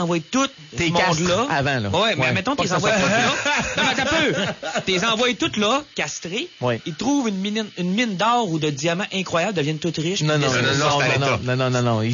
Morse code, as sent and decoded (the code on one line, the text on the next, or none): none